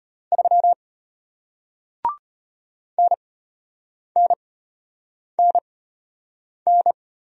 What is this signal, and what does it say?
Morse code, heard as a keyed tone, about 33 wpm, spaced as for 8 wpm, 700 Hz. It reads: 3 NNNN